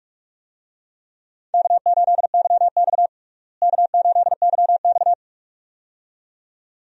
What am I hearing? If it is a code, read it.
Morse code, heard as a keyed tone, about 46 wpm, 700 Hz. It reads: K8YX K8YX
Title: USA Call Signs - 4 Characters - Repeated 2x - 45wpm